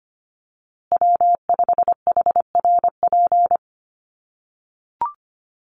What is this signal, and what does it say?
Morse code, heard as a keyed tone, about 25 wpm, 700 Hz. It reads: W5HRP
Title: USA Call Signs Repeated 2x 25wpm